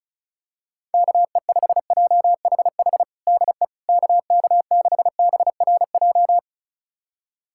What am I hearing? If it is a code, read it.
Morse code, heard as a keyed tone, about 34 wpm, 700 Hz. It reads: KE5JHH DE KK6BRJ
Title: USA Callsign DE Callsign - 6 Characters - 35wpm